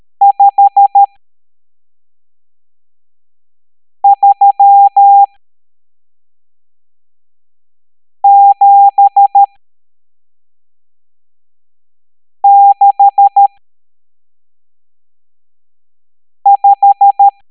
Morse code, heard as a keyed tone, about 13 wpm, spaced as for 3 wpm, 800 Hz. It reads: 53765
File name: hamdata / L12-EX02-CHIFFRES